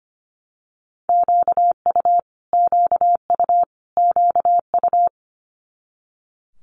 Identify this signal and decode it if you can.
Morse code, heard as a keyed tone, about 25 wpm, 700 Hz. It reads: QU QU QU
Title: Multi-letter Phonemes Repeated 3x 25wpm